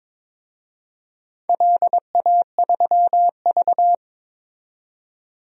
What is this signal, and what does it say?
Morse code, heard as a keyed tone, about 22 wpm, 700 Hz. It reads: LA3V
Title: DXCC-100 Call Signs - 4 Characters - 22wpm